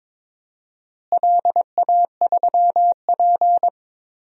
Morse code, heard as a keyed tone, about 21 wpm, 700 Hz. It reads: LA3P
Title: DXCC-100 Call Signs - 4 Characters - Repeated 3x - 22wpm